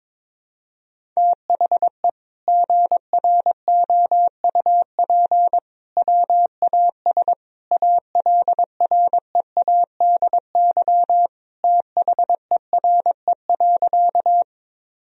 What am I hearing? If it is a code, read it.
Morse code, heard as a keyed tone, about 22 wpm, 700 Hz. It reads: THE GROUP WAS ALREADY THERE.
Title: Sentences from Top 1000 Words - Review Part 1 - 22wpm